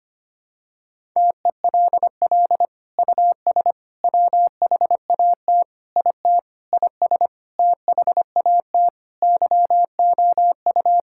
Morse code, heard as a keyed tone, about 25 wpm, 700 Hz. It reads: TELL US WHAT IT IS THAT YOU
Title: Sentences from Top 1000 Words - Review Part 6 - 25wpm